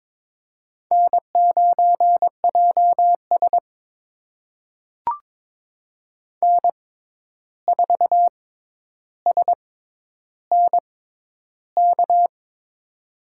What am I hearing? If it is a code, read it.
Morse code, heard as a keyed tone, about 22 wpm, spaced as for 11 wpm, 700 Hz. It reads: N9JS N4SNK
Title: USA Call Signs - 6x Character Spacing - 22wpm